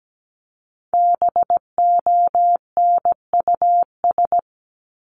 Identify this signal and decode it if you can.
Morse code, heard as a keyed tone, about 17 wpm, 700 Hz. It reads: BONUS
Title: Top 1751-2000 Words 17wpm